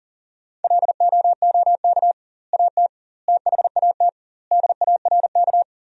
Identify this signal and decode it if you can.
Morse code, heard as a keyed tone, about 40 wpm, 700 Hz. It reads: LOOK AT THAT DARK
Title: 2x Word-Spacing - Sentences from Top 500 Words - 40wpm